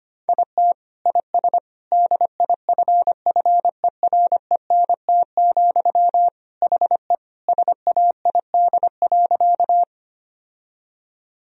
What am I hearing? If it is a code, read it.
Morse code, heard as a keyed tone, about 25 wpm, 700 Hz. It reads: IT IS DIFFERENT, HE SAID.
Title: Sentences from Top 1000 Words - Review Part 3 - 25wpm